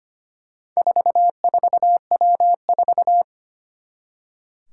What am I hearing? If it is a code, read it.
Morse code, heard as a keyed tone, about 25 wpm, 700 Hz. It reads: 44W4
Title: Four Challenging Letter-Number Combinations 25wpm